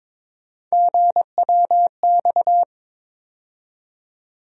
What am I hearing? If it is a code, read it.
Morse code, heard as a keyed tone, about 22 wpm, 700 Hz. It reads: GWX
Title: USA Call Sign Postfixes - 3 Letters 22wpm